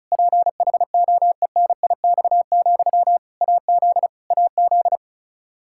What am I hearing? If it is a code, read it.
Morse code, heard as a keyed tone, about 36 wpm, 700 Hz. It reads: PHOENIX, AZ AZ
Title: Practice QSO - Exchange 1 - Beginning - 35wpm